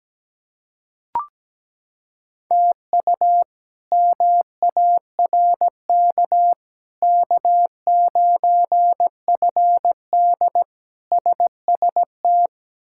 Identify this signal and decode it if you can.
Morse code, heard as a keyed tone, about 17 wpm, 700 Hz. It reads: TU MARK K9FD SST
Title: K1USN SST QSO Practice - 17wpm